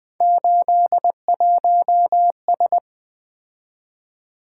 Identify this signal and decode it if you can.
Morse code, heard as a keyed tone, about 20 wpm, 700 Hz. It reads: 81S
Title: Three Letter-Number Combinations - 20wpm